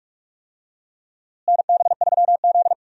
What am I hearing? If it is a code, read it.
Morse code, heard as a keyed tone, about 46 wpm, 700 Hz. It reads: NB3Z
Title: USA Call Signs - 4 Characters - Repeated 3x - 45wpm